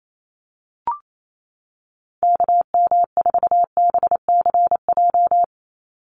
Morse code, heard as a keyed tone, about 28 wpm, 700 Hz. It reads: KM4BCJ
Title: USA Call Signs 28wpm